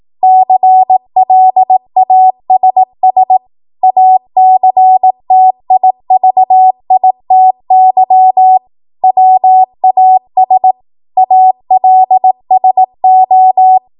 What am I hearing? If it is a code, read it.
Morse code, heard as a keyed tone, about 18 wpm, 750 Hz. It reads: CLASS ACTIVITY WAS ALSO
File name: Morse / 18WPM